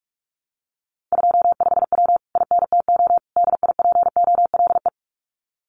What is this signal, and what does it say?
Morse code, heard as a keyed tone, about 46 wpm, 700 Hz. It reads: 25W INTO DIPOLE
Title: Common QSO Phrases Encore 45wpm